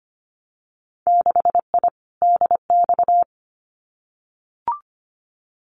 Morse code, heard as a keyed tone, about 25 wpm, 690 Hz. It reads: 6I DX
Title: Winter Field Day Exchanges Encore 25wpm